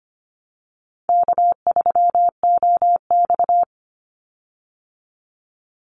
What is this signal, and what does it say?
Morse code, heard as a keyed tone, about 25 wpm, 700 Hz. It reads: K3OX
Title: Call Signs Repeated 2x - 25wpm